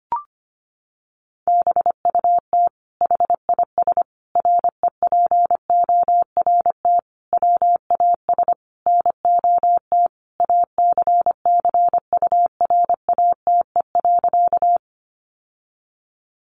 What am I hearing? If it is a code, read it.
Morse code, heard as a keyed tone, about 25 wpm, 700 Hz. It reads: BUT HIS REPORT WAS NOT ACCURATE.
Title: Sentences from Top 1200 Words 25wpm